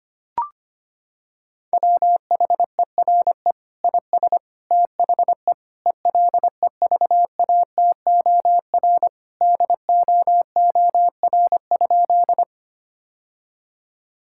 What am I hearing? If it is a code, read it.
Morse code, heard as a keyed tone, about 25 wpm, 700 Hz. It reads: WHERE IS THE ELEVATOR DOOR?
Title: Sentences from Top 1600 Words 25wpm